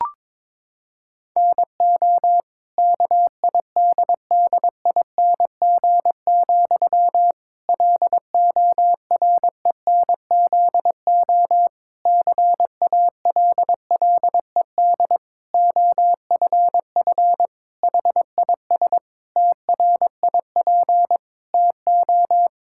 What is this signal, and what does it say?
Morse code, heard as a keyed tone, about 22 wpm, 700 Hz. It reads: NO KIDDING, LORENZO CALLED OFF HIS TRIP TO
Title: Pangram Challenge v2 - 22wpm